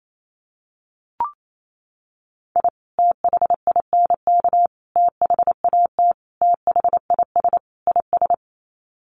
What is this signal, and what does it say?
Morse code, heard as a keyed tone, about 28 wpm, 700 Hz. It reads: I THINK THAT THIS IS
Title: Sets of 5 Words Encore2 28wpm